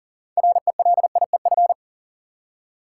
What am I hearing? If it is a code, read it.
Morse code, heard as a keyed tone, about 40 wpm, 700 Hz. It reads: RELIEF
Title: Six Letter Words - 40wpm